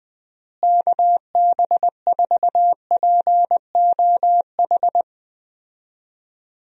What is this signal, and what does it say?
Morse code, heard as a keyed tone, about 20 wpm, 700 Hz. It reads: KB4POH
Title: USA Call Signs - 6 Characters - Repeated 3x - 20wpm